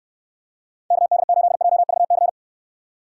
Morse code, heard as a keyed tone, about 69 wpm, 700 Hz. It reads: KD8JVC